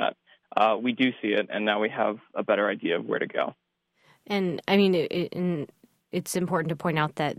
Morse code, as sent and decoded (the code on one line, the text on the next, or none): none